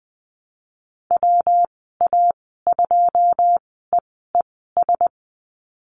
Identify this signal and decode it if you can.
Morse code, heard as a keyed tone, about 19 wpm, spaced as for 14 wpm, 700 Hz. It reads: WA2EES
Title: 140 Lesson 23g - Letter f - Call Signs 20wpm